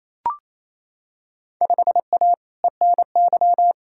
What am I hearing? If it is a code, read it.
Morse code, heard as a keyed tone, about 28 wpm, 700 Hz. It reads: 5A ENY
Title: Field Day Exchanges Encore - 28wpm